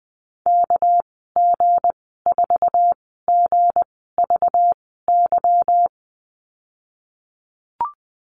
Morse code, heard as a keyed tone, about 20 wpm, spaced as for 15 wpm, 700 Hz. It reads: KG4GVY
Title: 207 Lesson 33g - Letter j - Call Signs 20wpm